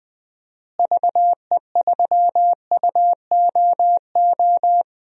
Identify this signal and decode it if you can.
Morse code, heard as a keyed tone, about 20 wpm, 700 Hz. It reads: VE3UOO
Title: DXCC-100 Call Signs - 6 Characters - Repeated 3x - 20wpm